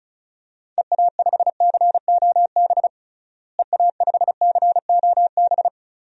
Morse code, heard as a keyed tone, about 34 wpm, 700 Hz. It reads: EA5COB EA5COB
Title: DXCC-100 Call Signs - 6 Characters - Repeated 3x - 35wpm